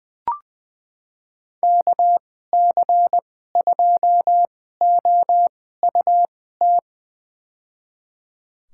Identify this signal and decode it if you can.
Morse code, heard as a keyed tone, about 20 wpm, spaced as for 14 wpm, 700 Hz. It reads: KC2OUT
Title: USA Call Signs - 6 Characters - 2x Character Spacing - 20wpm